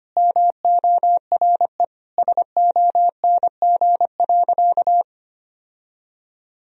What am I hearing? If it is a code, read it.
Morse code, heard as a keyed tone, about 25 wpm, 700 Hz. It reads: MORE SONG.
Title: Sentences from Top 1000 Words - Review Part 2 - 25wpm